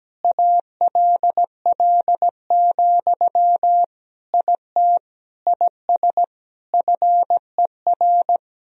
Morse code, heard as a keyed tone, about 17 wpm, 700 Hz. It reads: ALL, IT IS FER
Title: Sentences from Top 500 Words and Abbreviations 17wpm